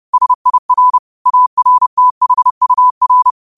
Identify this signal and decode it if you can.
Morse code, heard as a keyed tone, about 30 wpm, 1000 Hz. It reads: SIR ARTHUR
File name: qotdfiles / 30wpm